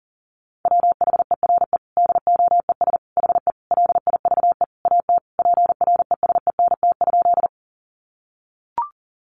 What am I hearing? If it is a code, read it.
Morse code, heard as a keyed tone, about 40 wpm, 700 Hz. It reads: WHERE DOES HE LIVE AT PRESENT?
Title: Sentences from Top 800 Words 40wpm